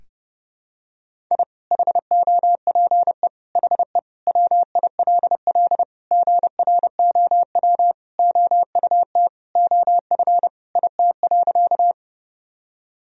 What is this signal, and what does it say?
Morse code, heard as a keyed tone, about 30 wpm, 700 Hz. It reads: I HOPE HE WILL GROW OUT OF IT.
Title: Sentences from Top 1000 Words - Review Part 1 - 30wpm